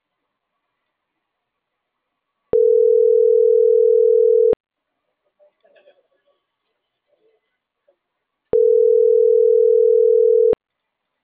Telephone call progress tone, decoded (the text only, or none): ringback tone